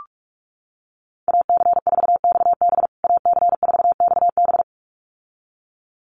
Morse code, heard as a keyed tone, about 44 wpm, 700 Hz. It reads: AC4XB AC4XB
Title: USA Call Signs Repeated 2x 45wpm